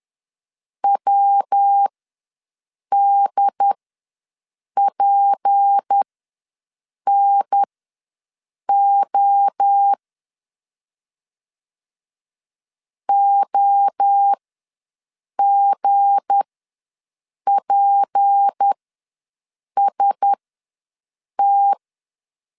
Morse code, heard as a keyed tone, about 10 wpm, spaced as for 6 wpm, 800 Hz. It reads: WDPNO OGPST
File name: LEZIONI / 7 GF9Z / 7 G ws 10wpm 3843